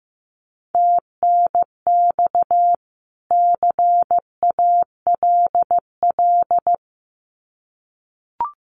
Morse code, heard as a keyed tone, about 15 wpm, 700 Hz. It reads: TNX CALL
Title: Common QSO Phrases 15wpm